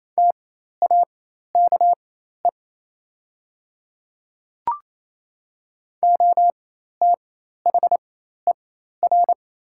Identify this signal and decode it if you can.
Morse code, heard as a keyed tone, about 28 wpm, spaced as for 13 wpm, 690 Hz. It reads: TAKE OTHER
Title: 4x Character Spacing - Top 300 Words - 28wpm